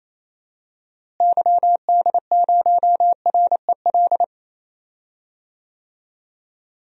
Morse code, heard as a keyed tone, about 27 wpm, 700 Hz. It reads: YD0REL
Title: DXCC-100 Call Signs - 6 Characters - Repeated 2x - 28wpm